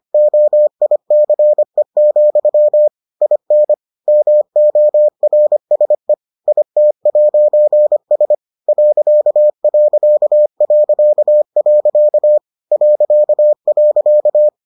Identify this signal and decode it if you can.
Morse code, heard as a keyed tone, about 25 wpm, 600 Hz. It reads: OICE, IN MORSE IT'S .... ..